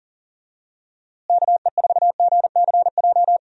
Morse code, heard as a keyed tone, about 40 wpm, 700 Hz. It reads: KE4GCJ
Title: USA Call Signs - 6 Characters - Repeated 3x - 40wpm